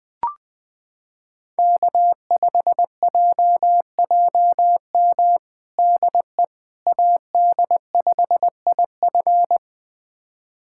Morse code, heard as a keyed tone, about 20 wpm, 700 Hz. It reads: K5JJM DE AD5IF
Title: USA Callsign DE Callsign - 5 Characters - 20wpm